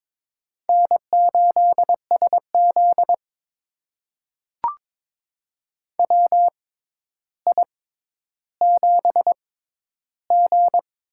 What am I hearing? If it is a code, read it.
Morse code, heard as a keyed tone, about 22 wpm, spaced as for 11 wpm, 700 Hz. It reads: N8SZ WI7G
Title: USA Call Signs - 4 Characters - 6x Character Spacing - 22wpm